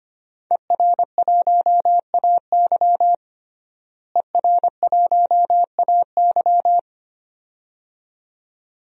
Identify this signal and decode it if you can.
Morse code, heard as a keyed tone, about 25 wpm, 700 Hz. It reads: ER1AY ER1AY
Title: DXCC-100 Call Signs - 5 Characters - Repeated 2x - 25wpm